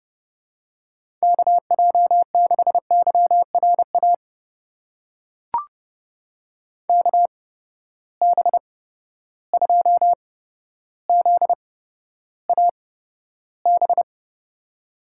Morse code, heard as a keyed tone, about 31 wpm, spaced as for 13 wpm, 700 Hz. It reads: KJ6YRA KB2ZAB
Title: USA Call Signs - 6 Characters - 8x Character Spacing - 30wpm